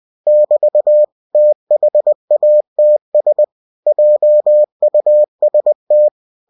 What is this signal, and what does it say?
Morse code, heard as a keyed tone, about 20 wpm, 600 Hz. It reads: <BT> THATS JUST
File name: cw-books / winnie-the-pooh / winnie-20wpm-page-0011